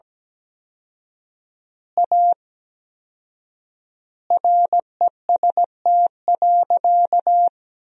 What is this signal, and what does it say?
Morse code, heard as a keyed tone, about 17 wpm, 700 Hz. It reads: A REST.
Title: 4x Word-Spacing - Sentences from Top 500 Words - 17wpm